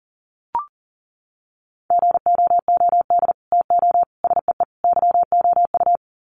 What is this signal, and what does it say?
Morse code, heard as a keyed tone, about 40 wpm, 700 Hz. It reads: GOOD TO SEE YOU
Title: Sets of 4 Words 40wpm